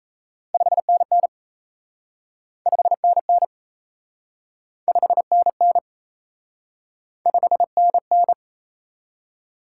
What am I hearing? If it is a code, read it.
Morse code, heard as a keyed tone, about 38 wpm, 700 Hz. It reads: 5NN 5NN 5NN 5NN